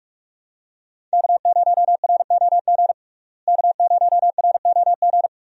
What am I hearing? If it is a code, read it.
Morse code, heard as a keyed tone, about 46 wpm, 700 Hz. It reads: K0ROG K0ROG
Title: USA Call Signs - 5 Characters - Repeated 3x - 45wpm